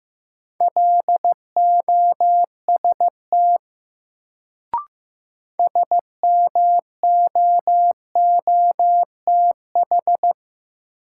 Morse code, heard as a keyed tone, about 15 wpm, 700 Hz. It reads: LOST SMOOTH
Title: Sentences from Top 1400 Words - New Word Drill - 15wpm